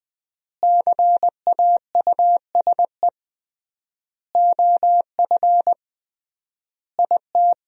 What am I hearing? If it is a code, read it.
Morse code, heard as a keyed tone, about 20 wpm, 700 Hz. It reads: CAUSE OF IT
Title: 3x Word-Spacing - Sentences from Top 200 Words - 20wpm